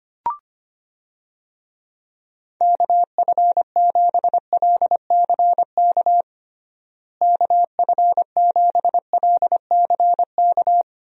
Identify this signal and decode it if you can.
Morse code, heard as a keyed tone, about 25 wpm, 700 Hz. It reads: KF7LCK KF7LCK
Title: USA Call Signs - 6 Characters - Repeated 3x - 25wpm